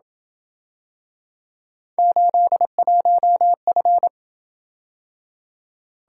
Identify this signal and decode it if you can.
Morse code, heard as a keyed tone, about 26 wpm, 700 Hz. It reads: 81F